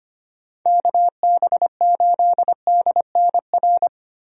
Morse code, heard as a keyed tone, about 25 wpm, 700 Hz. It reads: KB8DNR
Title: USA Call Signs - 6 Characters - Repeated 2x - 25wpm